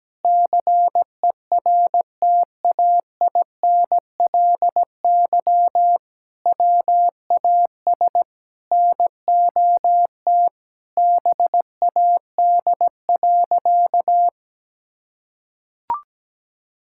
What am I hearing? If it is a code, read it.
Morse code, heard as a keyed tone, about 17 wpm, 700 Hz. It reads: CERTAINLY WAS NOT BAD.
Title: Sentences from Top 1000 Words - Review Part 6 - 17wpm